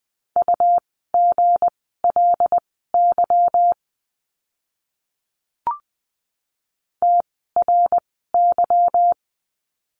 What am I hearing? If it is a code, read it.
Morse code, heard as a keyed tone, about 20 wpm, spaced as for 15 wpm, 700 Hz. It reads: UGLY TRY
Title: 160 Lesson 26f - Letter g - Words 20wpm